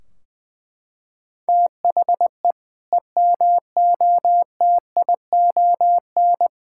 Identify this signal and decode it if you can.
Morse code, heard as a keyed tone, about 20 wpm, 700 Hz. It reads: THE EMOTION
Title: Sentences from Top 1600 Words 20wpm